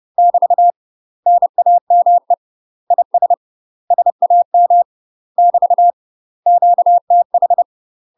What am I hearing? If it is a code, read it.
Morse code, heard as a keyed tone, about 30 wpm, 700 Hz. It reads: <BT> NAME IS SAM <BT> QTH
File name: UK MASTER QSOs TRAINING / 30WPM / UK MASTER QSOs 0045